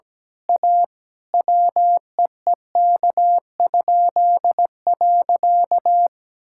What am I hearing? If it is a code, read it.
Morse code, heard as a keyed tone, about 17 wpm, 700 Hz. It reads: A WEEK?.
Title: Pangram Challenge v2 - 17wpm